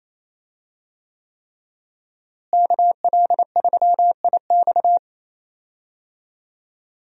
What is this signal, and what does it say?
Morse code, heard as a keyed tone, about 28 wpm, 700 Hz. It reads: KL3IX
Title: USA Call Signs - 5 Characters - Repeated 3x - 28wpm